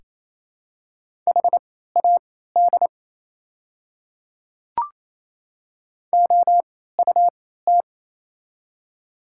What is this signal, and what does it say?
Morse code, heard as a keyed tone, about 28 wpm, spaced as for 16 wpm, 700 Hz. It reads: HAD OUT
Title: 3x Character Spacing - Top 100 Words - 28wpm